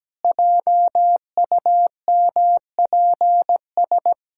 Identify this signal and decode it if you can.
Morse code, heard as a keyed tone, about 17 wpm, 700 Hz. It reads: JUMPS
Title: Pangram Challenge v2 - 17wpm